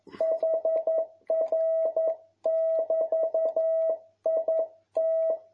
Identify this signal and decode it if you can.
Morse code, heard as a keyed tone, about 11 wpm, 650 Hz. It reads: HR<BT>IT
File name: jh8cbh / 2021cw